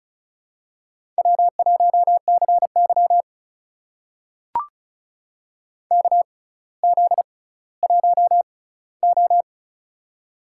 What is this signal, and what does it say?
Morse code, heard as a keyed tone, about 36 wpm, spaced as for 18 wpm, 700 Hz. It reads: W1CY KZ1O
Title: USA Call Signs - 4 Characters - 6x Character Spacing - 35wpm